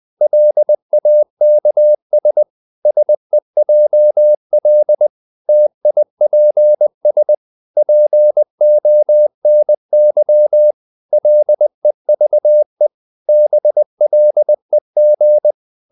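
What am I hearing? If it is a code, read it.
Morse code, heard as a keyed tone, about 20 wpm, 600 Hz. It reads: LAKS SEJL TIPS PONY LEVE BLEG
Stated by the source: Morse code practice of short Danish words at 20 wpm, from CW med Gnister.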